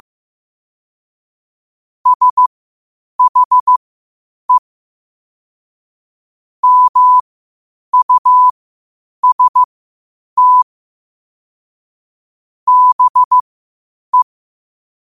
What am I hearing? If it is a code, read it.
Morse code, heard as a keyed tone, about 15 wpm, spaced as for 8 wpm, 1000 Hz. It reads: SHE MUST BE
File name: qotdfiles / 8wpm